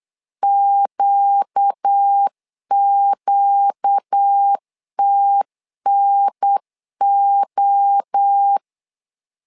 Morse code, heard as a keyed tone, about 8 wpm, 800 Hz. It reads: QQTNO
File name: LEZIONI / 4 Q 8wpm 2732